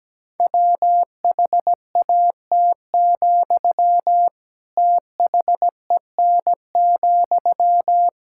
Morse code, heard as a keyed tone, about 17 wpm, 700 Hz. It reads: WHAT, THEN,
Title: Sentences from Top 1000 Words - Review Part 6 - 17wpm